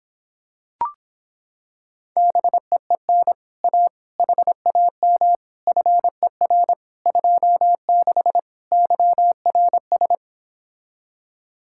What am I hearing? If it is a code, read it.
Morse code, heard as a keyed tone, about 26 wpm, 700 Hz. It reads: BEEN A HAM FER 26 YRS